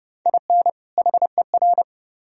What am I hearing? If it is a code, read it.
Morse code, heard as a keyed tone, about 27 wpm, 700 Hz. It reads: IN HER